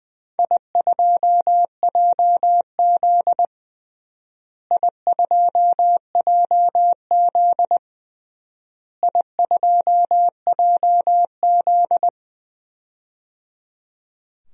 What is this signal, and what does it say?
Morse code, heard as a keyed tone, about 20 wpm, 700 Hz. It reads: I2JZ I2JZ I2JZ
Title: DXCC-100 Call Signs - 4 Characters - Repeated 3x - 20wpm